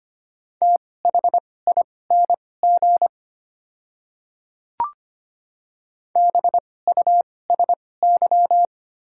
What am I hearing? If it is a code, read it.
Morse code, heard as a keyed tone, about 25 wpm, spaced as for 18 wpm, 700 Hz. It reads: THING BUSY
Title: 2x Character Spacing - Top 500 Words - 25wpm